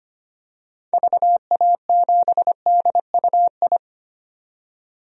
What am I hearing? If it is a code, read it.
Morse code, heard as a keyed tone, about 25 wpm, 700 Hz. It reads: VA7DUI
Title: DXCC-100 Call Signs - 6 Characters - 25wpm